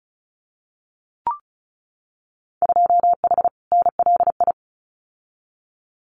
Morse code, heard as a keyed tone, about 36 wpm, 700 Hz. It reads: 2H NLI
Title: Winter Field Day Exchanges 35wpm